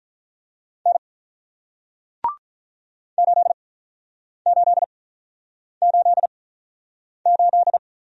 Morse code, heard as a keyed tone, about 48 wpm, spaced as for 9 wpm, 700 Hz. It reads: N 8888